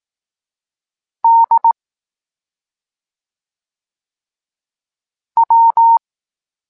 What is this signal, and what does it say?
Morse code, heard as a keyed tone, about 18 wpm, 920 Hz. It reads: D W
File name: Tekst-23 4